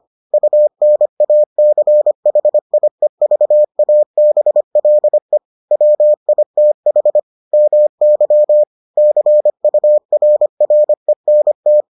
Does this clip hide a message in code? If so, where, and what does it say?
Yes: Morse code from 0.3 to 11.8 s, heard as a keyed tone, about 25 wpm, 600 Hz. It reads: UNACHIEVABLE WITH MY CURRENT